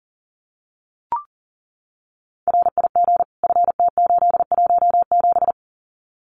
Morse code, heard as a keyed tone, about 40 wpm, 700 Hz. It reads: RIG FT817
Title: Common QSO Phrases Encore 40wpm